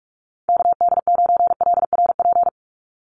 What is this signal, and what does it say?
Morse code, heard as a keyed tone, about 44 wpm, 700 Hz. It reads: KD9LRP
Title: USA Call Signs 45wpm